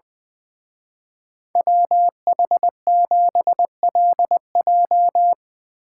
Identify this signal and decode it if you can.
Morse code, heard as a keyed tone, about 22 wpm, 700 Hz. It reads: WH7LJ
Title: Speed Racing - USA 2x2 Call Signs 20wpm